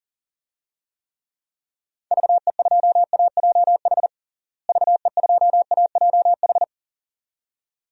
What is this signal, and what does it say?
Morse code, heard as a keyed tone, about 40 wpm, 700 Hz. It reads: VE2AJH VE2AJH
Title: DXCC-100 Call Signs - 6 Characters - Repeated 2x - 40wpm